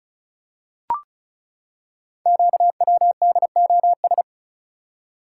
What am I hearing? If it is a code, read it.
Morse code, heard as a keyed tone, about 36 wpm, 690 Hz. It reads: QWDOS
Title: Code Groups - 5 Letters - 35wpm